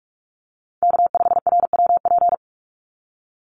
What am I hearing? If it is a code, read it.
Morse code, heard as a keyed tone, about 44 wpm, 700 Hz. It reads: K5RWP
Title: NA SOTA Chaser Callsigns 45wpm